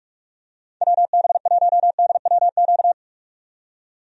Morse code, heard as a keyed tone, about 46 wpm, 700 Hz. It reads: WB1DWQ
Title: USA Call Signs - 6 Characters - 45wpm